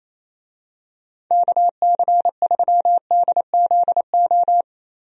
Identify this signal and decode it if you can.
Morse code, heard as a keyed tone, about 27 wpm, 700 Hz. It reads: KC3DZO
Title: USA Call Signs - 6 Characters - 28wpm